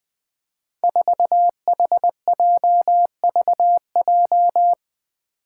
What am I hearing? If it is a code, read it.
Morse code, heard as a keyed tone, about 20 wpm, 700 Hz. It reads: 4HJVJ
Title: Five Challenging Letter-Number Combinations 20wpm